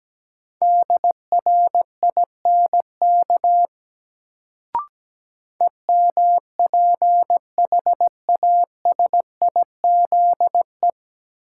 Morse code, heard as a keyed tone, about 17 wpm, 700 Hz. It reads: DRINK EMPHASIZE
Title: Sentences from Top 1500 Words - New Word Drill Alt - 17wpm